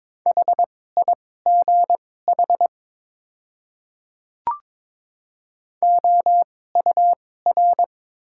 Morse code, heard as a keyed tone, about 22 wpm, spaced as for 16 wpm, 700 Hz. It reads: HIGH OUR
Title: 2x Character Spacing - Top 500 Words - 22wpm